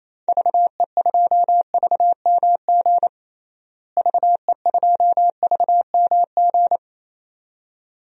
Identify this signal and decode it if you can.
Morse code, heard as a keyed tone, about 28 wpm, 700 Hz. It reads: VE2VMG VE2VMG
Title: DXCC-100 Call Signs - 6 Characters - Repeated 2x - 28wpm